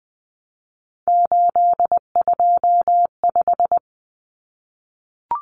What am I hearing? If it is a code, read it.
Morse code, heard as a keyed tone, about 20 wpm, 700 Hz. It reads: 825
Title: Common Cut Numbers - 3-Digit 20wpm